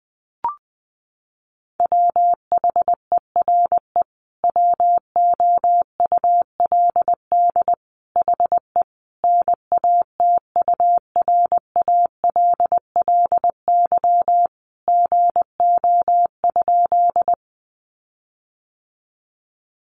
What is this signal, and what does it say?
Morse code, heard as a keyed tone, about 20 wpm, 700 Hz. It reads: WHERE WOULD HE NATURALLY GO?
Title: Sentences from Top 1300 Words 20wpm